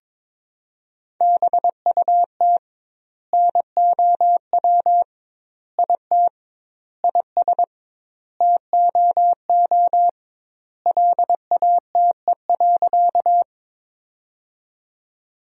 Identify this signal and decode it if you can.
Morse code, heard as a keyed tone, about 22 wpm, 700 Hz. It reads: BUT NOW IT IS TOO LATE.
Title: 2x Word-Spacing - Sentences from Top 300 Words - 22wpm